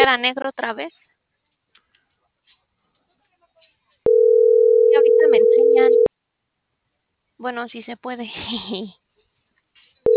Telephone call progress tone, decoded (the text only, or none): ringback tone